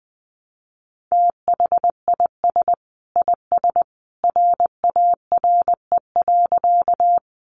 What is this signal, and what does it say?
Morse code, heard as a keyed tone, about 20 wpm, 700 Hz. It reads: THIS IS RARE.
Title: Sentences from Top 1100 Words 20wpm